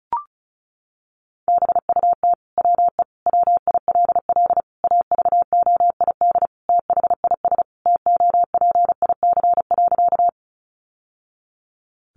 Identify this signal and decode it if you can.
Morse code, heard as a keyed tone, about 34 wpm, 700 Hz. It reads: BUT WE WILL AVOID THIS TOPIC.
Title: Sentences from Top 1200 Words 35wpm